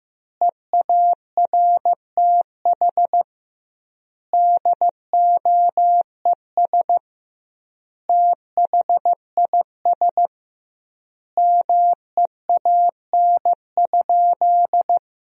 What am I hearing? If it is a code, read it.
Morse code, heard as a keyed tone, about 15 wpm, 700 Hz. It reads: EARTH DOES THIS MEAN?
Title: 2x Word-Spacing - Sentences from Top 200 Words - 15wpm